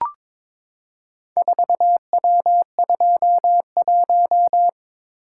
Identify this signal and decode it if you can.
Morse code, heard as a keyed tone, about 22 wpm, 700 Hz. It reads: 4W21
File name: practice / 22wpm / Four Letter-Number Combinations - 22wpm